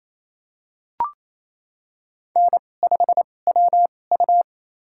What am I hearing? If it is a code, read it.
Morse code, heard as a keyed tone, about 28 wpm, spaced as for 20 wpm, 700 Hz. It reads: N5WU
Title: USA Call Signs - 4 Characters - 2x Character Spacing - 28wpm